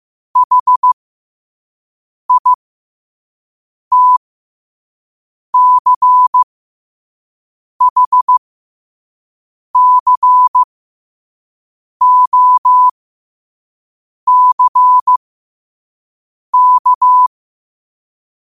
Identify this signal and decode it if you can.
Morse code, heard as a keyed tone, about 15 wpm, spaced as for 5 wpm, 1000 Hz. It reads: HITCHCOCK
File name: qotdfiles / 5wpm